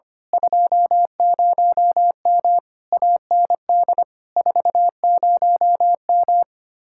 Morse code, heard as a keyed tone, about 25 wpm, 700 Hz. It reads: 20M AND 40M
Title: Speed-Racing Common QSO Phrases - 25wpm